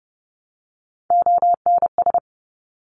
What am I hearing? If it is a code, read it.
Morse code, heard as a keyed tone, about 30 wpm, 700 Hz. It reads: ONS
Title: Top Three Letter Combinations in Words 30wpm